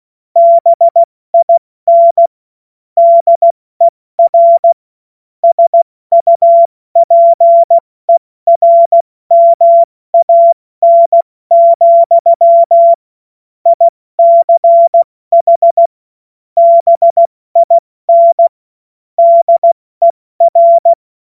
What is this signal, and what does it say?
Morse code, heard as a keyed tone, about 16 wpm, 680 Hz. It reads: BIN DER SUPERMAN, ICH BIN DER
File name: Morsekurs / Witze / Witz 24. August 2021, 70 F80